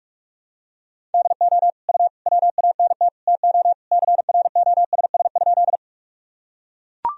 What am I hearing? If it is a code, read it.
Morse code, heard as a keyed tone, about 41 wpm, 700 Hz. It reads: DO U WANT TO CROSS?
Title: Sentences from Top 300 Words and Abbreviations 45wpm